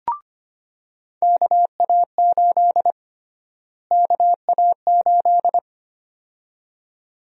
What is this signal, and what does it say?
Morse code, heard as a keyed tone, about 25 wpm, 700 Hz. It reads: KA8 KA8
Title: USA Call Sign Prefixes - Repeated 2x - 25wpm